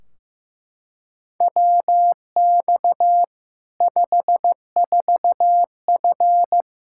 Morse code, heard as a keyed tone, about 15 wpm, 700 Hz. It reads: WX 54F
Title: K1USN SST QSO Practice 15wpm